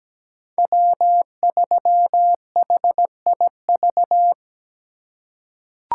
Morse code, heard as a keyed tone, about 17 wpm, 700 Hz. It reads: W3HIV
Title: USA Call Signs - 5 Characters - 4x Character Spacing - 17wpm